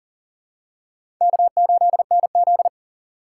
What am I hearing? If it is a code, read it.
Morse code, heard as a keyed tone, about 39 wpm, 700 Hz. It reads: K8NZ
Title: K1USN SST QSO Practice - 40wpm